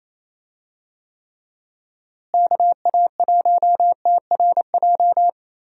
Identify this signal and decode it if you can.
Morse code, heard as a keyed tone, about 27 wpm, 700 Hz. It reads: KA1TRJ
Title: USA Call Signs - 6 Characters - Repeated 2x - 28wpm